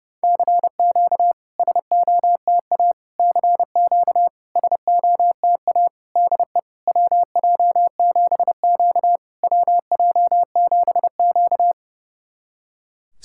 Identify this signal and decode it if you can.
Morse code, heard as a keyed tone, about 30 wpm, 700 Hz. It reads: CQ SOTA CQ SOTA DE WJ7Q WJ7Q
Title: SOTA QSO Practice 30wpm